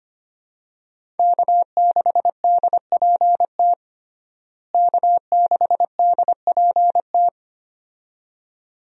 Morse code, heard as a keyed tone, about 25 wpm, 700 Hz. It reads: K6DPT K6DPT
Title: USA Call Signs - 5 Characters - Repeated 2x - 25wpm